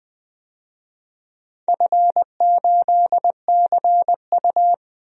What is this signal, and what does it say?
Morse code, heard as a keyed tone, about 20 wpm, 700 Hz. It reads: F8CU
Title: DXCC-100 Call Signs - 4 Characters - 20wpm